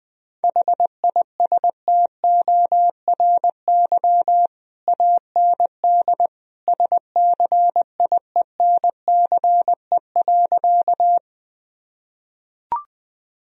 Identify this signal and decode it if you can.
Morse code, heard as a keyed tone, about 21 wpm, 700 Hz. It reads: HISTORY AND SCIENCE.
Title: Sentences from Top 1000 Words - Review Part 2 - 20wpm